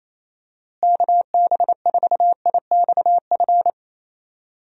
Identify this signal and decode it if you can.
Morse code, heard as a keyed tone, about 28 wpm, 700 Hz. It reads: KB4IXF
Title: USA Call Signs - 6 Characters - Repeated 3x - 28wpm